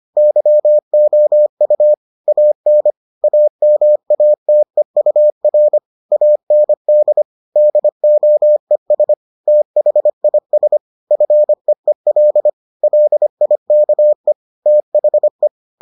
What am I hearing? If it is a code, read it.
Morse code, heard as a keyed tone, about 25 wpm, 600 Hz. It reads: YOU AN AMATEUR AND DOES THIS FEEL LIKE THE